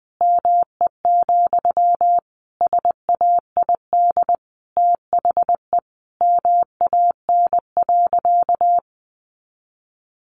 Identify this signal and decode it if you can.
Morse code, heard as a keyed tone, about 20 wpm, 700 Hz. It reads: ME, SAID THE MAN.
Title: Sentences from Top 200 Words 20wpm